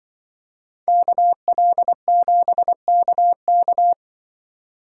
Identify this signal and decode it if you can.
Morse code, heard as a keyed tone, about 24 wpm, 700 Hz. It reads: KL7KK
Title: Speed Racing - USA Call Signs 20wpm